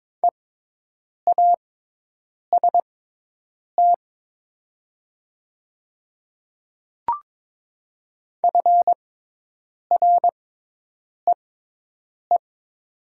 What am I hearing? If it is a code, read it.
Morse code, heard as a keyed tone, about 22 wpm, spaced as for 8 wpm, 700 Hz. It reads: EAST FREE
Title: 6x Character Spacing - Top 500 Words - 22wpm